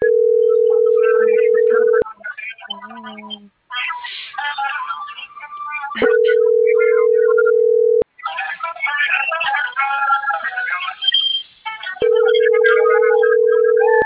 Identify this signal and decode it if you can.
Ringback tone (North America), 440 + 480 Hz, with noise.